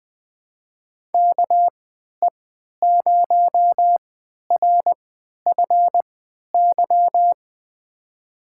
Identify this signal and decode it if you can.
Morse code, heard as a keyed tone, about 20 wpm, spaced as for 11 wpm, 700 Hz. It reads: KE0RFY
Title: USA Call Signs - 6 Characters - 3x Character Spacing - 20wpm